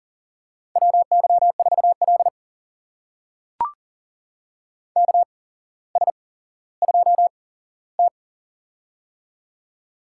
Morse code, heard as a keyed tone, about 38 wpm, spaced as for 17 wpm, 700 Hz. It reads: WY4L KS2T
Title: USA Call Signs - 4 Characters - 8x Character Spacing - 40wpm